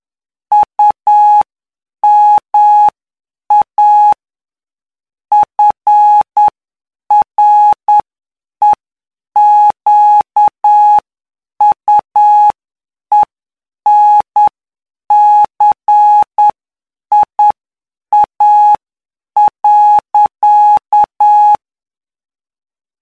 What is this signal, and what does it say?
Morse code, heard as a keyed tone, about 9 wpm, spaced as for 7 wpm, 840 Hz. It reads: UMA FREQUENCIA.
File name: cw / py2lcd / 19-PY2LCD-Li��o 28